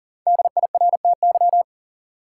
Morse code, heard as a keyed tone, about 38 wpm, 700 Hz. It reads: DIRTY
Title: Five Letter Words - 40wpm